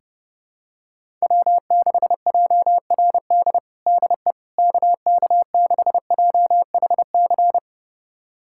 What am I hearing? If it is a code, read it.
Morse code, heard as a keyed tone, about 30 wpm, 700 Hz. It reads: W6JRD DE KK6JHC
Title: USA Callsign DE Callsign 30wpm